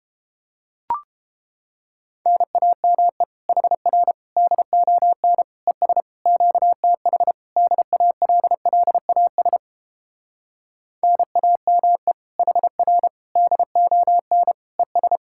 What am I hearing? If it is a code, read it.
Morse code, heard as a keyed tone, about 29 wpm, 700 Hz. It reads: NAME HR DON ES QTH DALLAS NAME HR DON ES